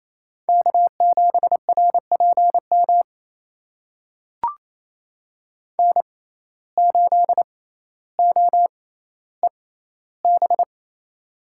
Morse code, heard as a keyed tone, about 28 wpm, spaced as for 14 wpm, 700 Hz. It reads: K7RPM N8OEB
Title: USA Call Signs - 5 Characters - 6x Character Spacing - 28wpm